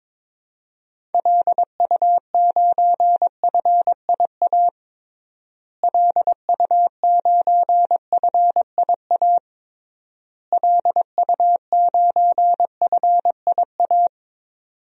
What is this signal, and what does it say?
Morse code, heard as a keyed tone, about 22 wpm, 700 Hz. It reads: LU9FIA LU9FIA LU9FIA
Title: DXCC-100 Call Signs - 6 Characters - Repeated 3x - 22wpm